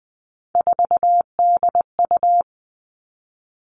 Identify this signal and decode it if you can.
Morse code, heard as a keyed tone, about 20 wpm, 700 Hz. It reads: 4DU
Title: All Cut Numbers - 3-Digit 20wpm